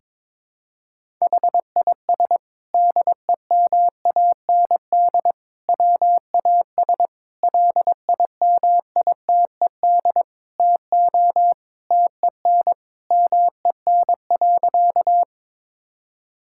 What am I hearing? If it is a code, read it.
Morse code, heard as a keyed tone, about 22 wpm, 700 Hz. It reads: HIS DEMAND WAS LIMITED TO TEN MEN.
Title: Sentences from Top 1000 Words - Review Part 6 - 22wpm